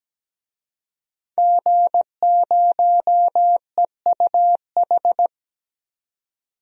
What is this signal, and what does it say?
Morse code, heard as a keyed tone, about 17 wpm, 700 Hz. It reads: G0EUH
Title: DXCC-100 Call Signs - 5 Characters - 17wpm